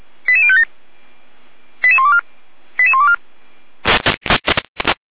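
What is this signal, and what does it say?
Five-tone selective call (ZVEI), heard as quick runs of five tones, tones of about 70 ms. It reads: 88057 88124 88124
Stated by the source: ZVEI five-tone selective call received on an SDR.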